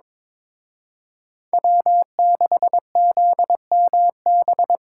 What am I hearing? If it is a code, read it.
Morse code, heard as a keyed tone, about 26 wpm, 700 Hz. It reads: W6ZMB
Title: Speed Racing - USA 1x3 Call Signs 22wpm